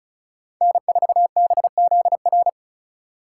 Morse code, heard as a keyed tone, about 34 wpm, 700 Hz. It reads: N4BZR
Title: USA Call Signs - 5 Characters - 3x Character Spacing - 35wpm